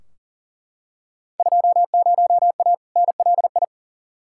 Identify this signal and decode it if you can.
Morse code, heard as a keyed tone, about 40 wpm, 700 Hz. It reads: 20A NLI
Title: Field Day Exchanges Encore - 40wpm